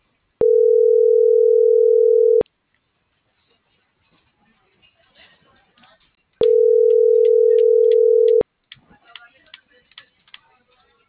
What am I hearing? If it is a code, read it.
Ringback tone (North America), 440 + 480 Hz.